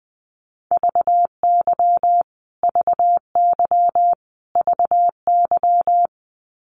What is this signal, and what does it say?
Morse code, heard as a keyed tone, about 19 wpm, 700 Hz. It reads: VY VY VY
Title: Abbreviations Repeated 3x - 20wpm